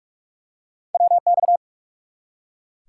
Morse code, heard as a keyed tone, about 44 wpm, 700 Hz. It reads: WX
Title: Top 180 QSO Elements 45wpm